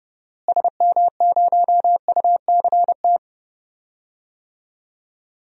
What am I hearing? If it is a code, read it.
Morse code, heard as a keyed tone, about 31 wpm, 700 Hz. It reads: SM0UCT
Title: DXCC-100 Call Signs - 6 Characters - Repeated 3x - 30wpm